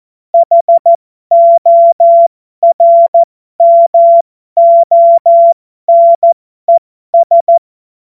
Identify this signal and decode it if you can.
Morse code, heard as a keyed tone, about 14 wpm, spaced as for 12 wpm, 680 Hz. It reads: HORMONES
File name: Morsekurs / Witze